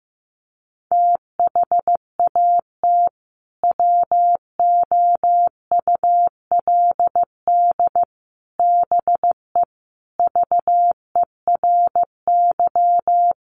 Morse code, heard as a keyed tone, about 15 wpm, 700 Hz. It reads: THAT WOULD BE VERY
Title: Sentences from Top 200 Words 15wpm